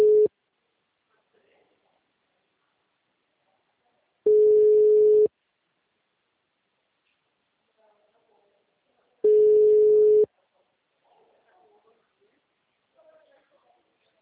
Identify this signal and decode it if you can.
Ringback tone (Europe, 425 Hz), with speech.